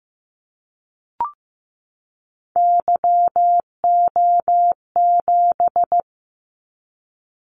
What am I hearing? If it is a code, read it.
Morse code, heard as a keyed tone, about 15 wpm, 700 Hz. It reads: YO7
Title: Callsign Prefixes for Top 25 DXCC - 15wpm